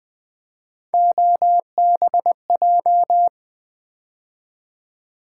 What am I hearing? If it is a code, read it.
Morse code, heard as a keyed tone, about 20 wpm, 700 Hz. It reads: OBJ
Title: Build A Word - 3 Letters At A Time - 9-Letter Words - 20wpm